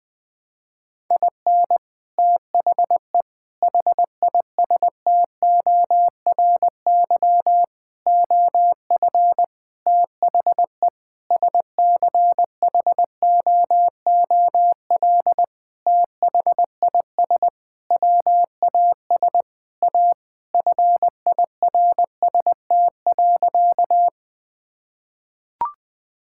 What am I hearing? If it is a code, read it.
Morse code, heard as a keyed tone, about 20 wpm, 700 Hz. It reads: IN THE HISTORY OF THE SCHOOL THIS WAS A FIRST.
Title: Sentences with 5-Word Phrases - Spoken Beginning - 20wpm